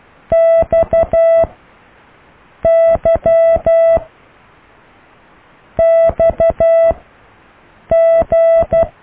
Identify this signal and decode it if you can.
Morse code, heard as a keyed tone, about 12 wpm, 660 Hz. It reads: X Y X G